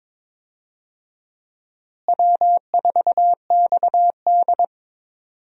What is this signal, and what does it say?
Morse code, heard as a keyed tone, about 22 wpm, 700 Hz. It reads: W4XD
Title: USA Call Signs - 4 Characters - Repeated 3x - 22wpm